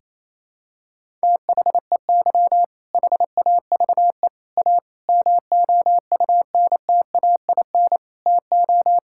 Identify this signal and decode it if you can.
Morse code, heard as a keyed tone, about 27 wpm, 700 Hz. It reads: THEY HAVE A MOUNTAIN TO
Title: Sentences from Top 1000 Words - Review Part 5 - 28wpm